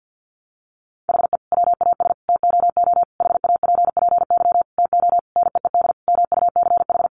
Morse code, heard as a keyed tone, about 50 wpm, 700 Hz. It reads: HE WAS TOO HAPPY TO NEED MUCH
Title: Sentences from Top 1000 Words 50wpm